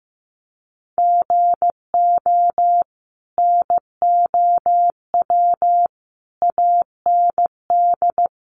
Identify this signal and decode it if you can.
Morse code, heard as a keyed tone, about 15 wpm, 700 Hz. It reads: GO NOW AND
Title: Sentences from Top 1500 Words 15wpm